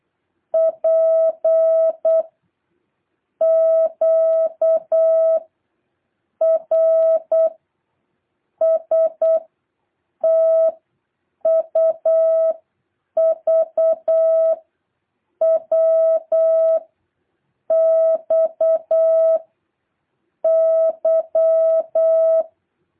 Morse code, heard as a keyed tone, about 8 wpm, 650 Hz. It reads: P Q R S T U V W X Y
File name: jh8cbh / 2023cw